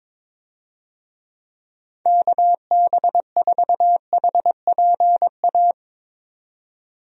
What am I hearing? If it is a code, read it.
Morse code, heard as a keyed tone, about 22 wpm, 700 Hz. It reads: KB4HPA
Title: USA Call Signs - 6 Characters - 22wpm